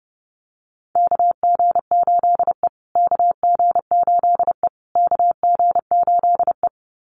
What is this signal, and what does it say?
Morse code, heard as a keyed tone, about 30 wpm, 700 Hz. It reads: KG8E KG8E KG8E
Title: USA Call Signs Repeated 3x 30wpm